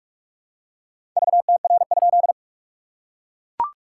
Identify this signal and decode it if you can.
Morse code, heard as a keyed tone, about 41 wpm, 690 Hz. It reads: VTR?
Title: SOTA QSO Practice 45wpm